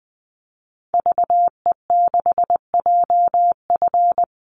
Morse code, heard as a keyed tone, about 20 wpm, 700 Hz. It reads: VE6JF
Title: Call Signs - 20wpm